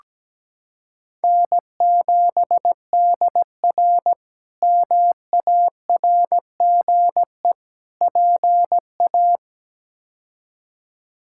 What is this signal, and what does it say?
Morse code, heard as a keyed tone, about 17 wpm, 700 Hz. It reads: N7DR MARGE PA
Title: K1USN SST QSO Practice - 17wpm